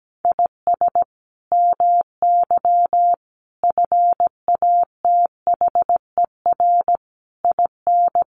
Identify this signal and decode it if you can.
Morse code, heard as a keyed tone, about 17 wpm, 700 Hz. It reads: IS MY FATHER IN
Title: Sentences from Top 500 Words 17wpm